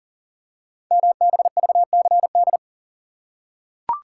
Morse code, heard as a keyed tone, about 39 wpm, 700 Hz. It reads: MBVCD
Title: Code Groups - 5 Letters - 40wpm